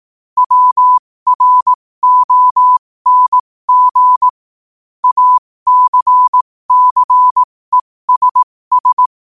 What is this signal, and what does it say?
Morse code, heard as a keyed tone, about 18 wpm, spaced as for 15 wpm, 1000 Hz. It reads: WRONG ACCESS